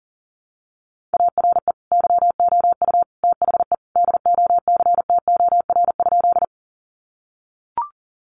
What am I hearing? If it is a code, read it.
Morse code, heard as a keyed tone, about 40 wpm, 700 Hz. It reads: ARE YOU THE DOCTOR?
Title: Sentences from Top 1500 Words 40wpm